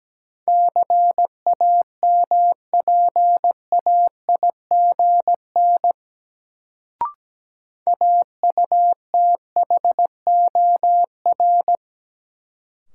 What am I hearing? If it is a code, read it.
Morse code, heard as a keyed tone, about 17 wpm, 700 Hz. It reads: CAMPAIGN AUTHOR
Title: Sentences from Top 1200 Words - New Word Drill - 17wpm